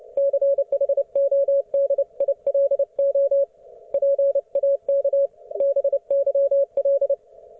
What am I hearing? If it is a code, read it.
Morse code, heard as a keyed tone, about 29 wpm, 560 Hz. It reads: CHODILO PAK BYL